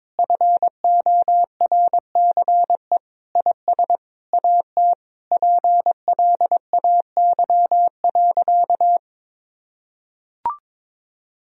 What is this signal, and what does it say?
Morse code, heard as a keyed tone, about 22 wpm, 700 Hz. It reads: FORCE IS AT PLAY.
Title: Sentences from Top 1000 Words - Review Part 4 - 22wpm